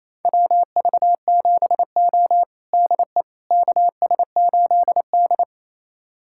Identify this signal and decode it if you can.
Morse code, heard as a keyed tone, about 28 wpm, 700 Hz. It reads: WV7O DE KS8D
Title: USA Callsign DE Callsign - 4 Characters - 28wpm